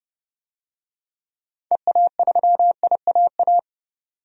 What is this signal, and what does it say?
Morse code, heard as a keyed tone, about 30 wpm, 700 Hz. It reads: EA3IAA
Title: DXCC-100 Call Signs - 6 Characters - 30wpm